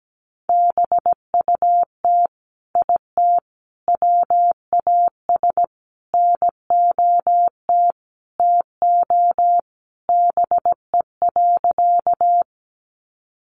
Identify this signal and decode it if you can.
Morse code, heard as a keyed tone, about 17 wpm, 700 Hz. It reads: BUT IT WAS NOT TO BE.
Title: Sentences from Top 600 Words 17wpm